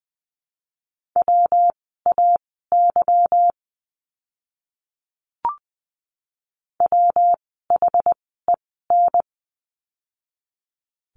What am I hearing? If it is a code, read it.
Morse code, heard as a keyed tone, about 20 wpm, spaced as for 15 wpm, 700 Hz. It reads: WAY WHEN
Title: 237 Lesson 38f - Letter q - Words 20wpm